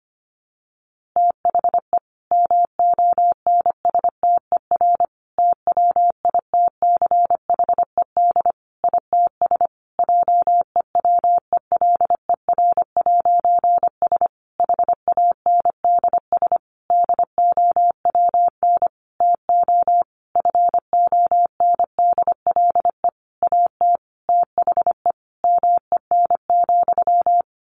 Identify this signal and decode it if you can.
Morse code, heard as a keyed tone, about 25 wpm, 700 Hz. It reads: THE MONSTER TWITCHED ITS JEWELER'S HANDS DOWN TO FONDLE AT THE MEN,
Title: A Sound Of Thunder 25wpm